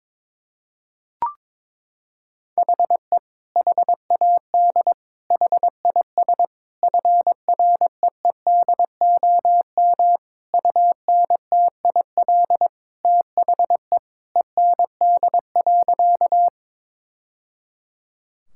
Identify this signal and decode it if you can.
Morse code, heard as a keyed tone, about 22 wpm, 700 Hz. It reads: HE HAD HIS FREEDOM UNTIL THE END.
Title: Sentences from Top 1000 Words - Review Part 5 - 22wpm